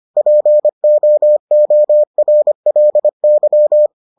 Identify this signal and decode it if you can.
Morse code, heard as a keyed tone, about 25 wpm, 600 Hz. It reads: POORLY